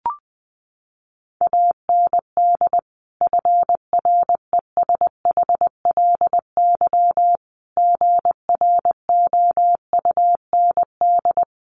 Morse code, heard as a keyed tone, about 20 wpm, 700 Hz. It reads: AND FRESHLY GROUND
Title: Sets of 5 Words Encore 20wpm